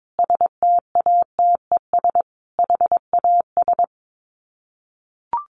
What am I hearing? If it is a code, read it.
Morse code, heard as a keyed tone, about 22 wpm, 700 Hz. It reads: STATES HAS
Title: Sets of 5 Words Encore 22wpm